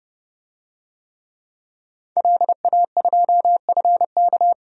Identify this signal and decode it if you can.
Morse code, heard as a keyed tone, about 31 wpm, 700 Hz. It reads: LA2FK
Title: DXCC-100 Call Signs - 5 Characters - 30wpm